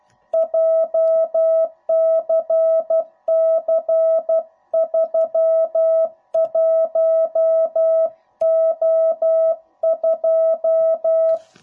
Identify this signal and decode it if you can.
Morse code, heard as a keyed tone, about 12 wpm, 660 Hz. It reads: JCC31O2
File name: jh8cbh / 2021cw / cw05b